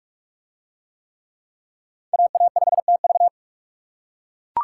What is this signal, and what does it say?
Morse code, heard as a keyed tone, about 42 wpm, 700 Hz. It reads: AA5TV